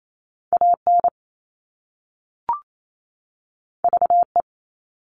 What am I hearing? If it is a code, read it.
Morse code, heard as a keyed tone, about 28 wpm, 700 Hz. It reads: AN VE
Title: All Cut Numbers - 2-Digit 28wpm